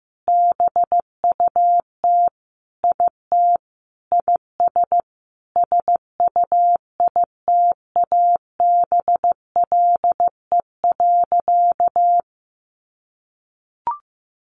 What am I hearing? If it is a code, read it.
Morse code, heard as a keyed tone, about 15 wpm, 700 Hz. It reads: BUT IT IS SUITABLE.
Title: Sentences from Top 1400 Words 15wpm